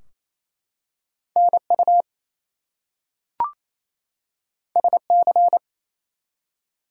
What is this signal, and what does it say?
Morse code, heard as a keyed tone, about 28 wpm, 700 Hz. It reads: NU SC
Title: US States and Canadian Provinces 28wpm